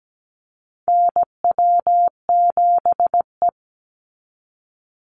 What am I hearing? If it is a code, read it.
Morse code, heard as a keyed tone, about 17 wpm, 700 Hz. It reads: NW7E
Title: NA SOTA Chaser Callsigns 17wpm